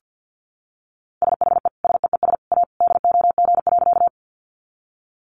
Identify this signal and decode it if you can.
Morse code, heard as a keyed tone, about 50 wpm, 700 Hz. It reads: SHE SEES A DOG.
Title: Easy Phrases 50wpm